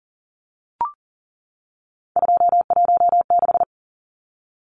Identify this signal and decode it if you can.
Morse code, heard as a keyed tone, about 40 wpm, 700 Hz. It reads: 216
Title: Numbers - 3-Digit 40wpm